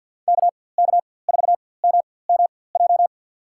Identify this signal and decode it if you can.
Morse code, heard as a keyed tone, about 53 wpm, spaced as for 24 wpm, 700 Hz. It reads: KK4MMJ
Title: USA Call Signs - 6 Characters - 4x Character Spacing - 50wpm